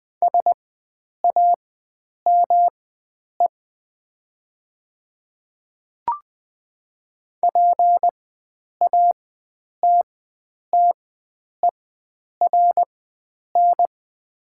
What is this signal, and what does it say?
Morse code, heard as a keyed tone, about 20 wpm, spaced as for 9 wpm, 700 Hz. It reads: SAME PATTERN
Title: 4x Character Spacing - Top 500 Words - 20wpm